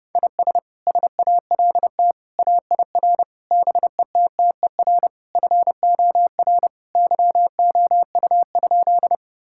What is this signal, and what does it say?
Morse code, heard as a keyed tone, about 32 wpm, 700 Hz. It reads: IS SALT AIR BETTER FOR YOU?
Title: Sentences from Top 1000 Words - Review Part 5 - 30wpm